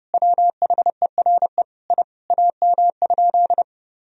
Morse code, heard as a keyed tone, about 30 wpm, 700 Hz. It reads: WHERE I AM?
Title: Sentences from Top 1000 Words - Review Part 2 - 30wpm